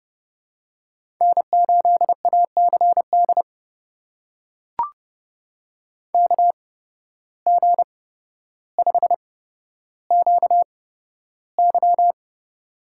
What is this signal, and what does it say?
Morse code, heard as a keyed tone, about 30 wpm, spaced as for 13 wpm, 700 Hz. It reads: N8ACD KG5QY
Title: USA Call Signs - 5 Characters - 8x Character Spacing - 30wpm